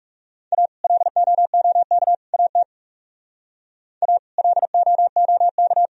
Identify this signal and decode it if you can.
Morse code, heard as a keyed tone, about 36 wpm, 700 Hz. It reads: A LOOK AT A LOOK
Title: Speed-Racing - Sets of 3 Words 30wpm